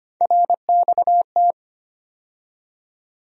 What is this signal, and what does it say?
Morse code, heard as a keyed tone, about 25 wpm, 700 Hz. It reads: RXT
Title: USA Call Sign Postfixes - 3 Letters 25wpm